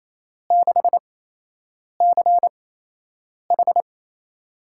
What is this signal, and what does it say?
Morse code, heard as a keyed tone, about 28 wpm, spaced as for 8 wpm, 700 Hz. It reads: 6CH
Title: USA Call Signs - 4 Characters - 8x Character Spacing - 28wpm